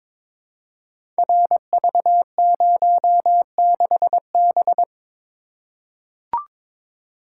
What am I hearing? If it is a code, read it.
Morse code, heard as a keyed tone, about 21 wpm, 700 Hz. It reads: RV06B
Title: Code Groups - 5 Letter:Numbers - 22wpm